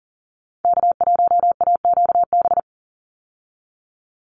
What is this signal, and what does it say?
Morse code, heard as a keyed tone, about 40 wpm, 700 Hz. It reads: K1AQB
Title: Call Signs Repeated 3x - 40wpm